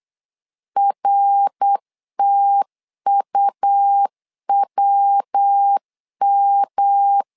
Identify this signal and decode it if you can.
Morse code, heard as a keyed tone, about 8 wpm, 800 Hz. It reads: RTUWM
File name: LEZIONI / 4 Q1S 8wpm 2710